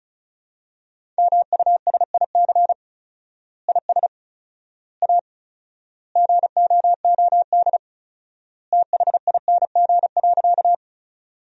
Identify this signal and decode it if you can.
Morse code, heard as a keyed tone, about 36 wpm, 700 Hz. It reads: MUSIC IS A GOOD THING.
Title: 4x Word-Spacing - Sentences from Top 300 Words - 35wpm